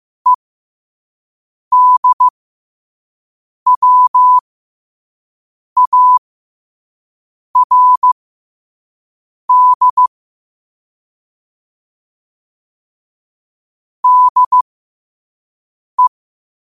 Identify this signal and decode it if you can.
Morse code, heard as a keyed tone, about 15 wpm, spaced as for 5 wpm, 1000 Hz. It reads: EDWARD DE
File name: qotdfiles / 5wpm